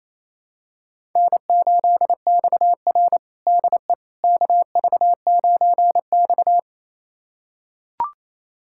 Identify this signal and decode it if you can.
Morse code, heard as a keyed tone, about 28 wpm, 700 Hz. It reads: N8XR DE KV9X
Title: USA Callsign DE Callsign - 4 Characters - 28wpm